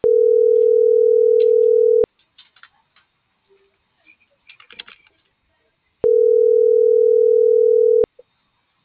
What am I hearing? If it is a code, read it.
Ringback tone (North America), 440 + 480 Hz.
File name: CENTRAL TERRESTRE / out claudiacruz